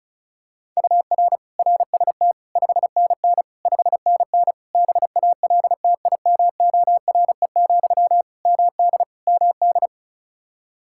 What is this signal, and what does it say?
Morse code, heard as a keyed tone, about 36 wpm, 700 Hz. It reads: UR RST 5NN 5NN BALTIMORE, MD MD
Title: Practice QSO - Exchange 1 - Beginning - 35wpm